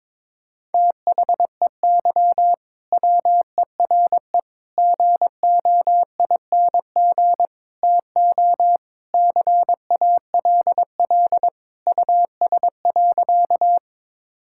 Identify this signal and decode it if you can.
Morse code, heard as a keyed tone, about 22 wpm, 700 Hz. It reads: THEY WERE GOING TO CALL US.
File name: practice / 22wpm / Sentences with 4-Word Phrases - 22wpm